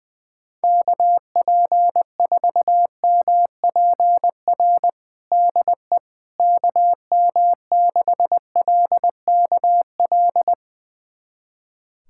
Morse code, heard as a keyed tone, about 20 wpm, 700 Hz. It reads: KP4MPR DE KM6LKL
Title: USA Callsign DE Callsign - 6 Characters - 20wpm